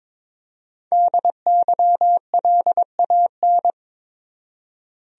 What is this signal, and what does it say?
Morse code, heard as a keyed tone, about 22 wpm, 700 Hz. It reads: DYLAN
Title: USA Common Names - Max 5 Letters - 22wpm